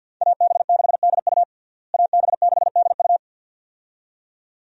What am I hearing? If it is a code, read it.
Morse code, heard as a keyed tone, about 53 wpm, 700 Hz. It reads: AB6DU AB6DU
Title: USA Call Signs - 5 Characters - Repeated 2x - 50wpm